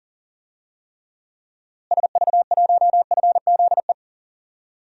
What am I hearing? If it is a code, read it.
Morse code, heard as a keyed tone, about 40 wpm, 700 Hz. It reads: SV1FZE
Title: DXCC-100 Call Signs - 6 Characters - Repeated 3x - 40wpm